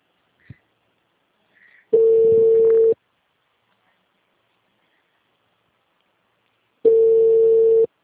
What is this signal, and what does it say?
Ringback tone (China, 450 Hz).
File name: out- -2038- - - .30002